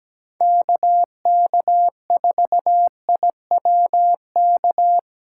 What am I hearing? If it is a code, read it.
Morse code, heard as a keyed tone, about 20 wpm, 700 Hz. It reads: KK4IWK
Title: Speed Racing - USA 2x3 Call Signs 17wpm